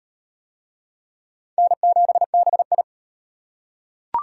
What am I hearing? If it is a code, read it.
Morse code, heard as a keyed tone, about 34 wpm, 700 Hz. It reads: N7BI